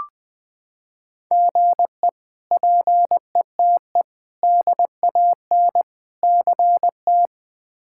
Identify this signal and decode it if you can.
Morse code, heard as a keyed tone, about 21 wpm, 700 Hz. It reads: GE PETE DAN CT
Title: K1USN SST QSO Practice - 20wpm